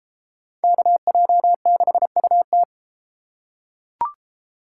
Morse code, heard as a keyed tone, about 32 wpm, 700 Hz. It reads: KJ6UT